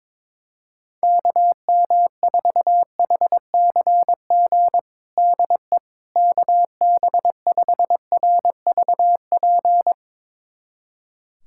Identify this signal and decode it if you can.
Morse code, heard as a keyed tone, about 21 wpm, 700 Hz. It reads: KM4HCG DE KB5RVP
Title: USA Callsign DE Callsign 22wpm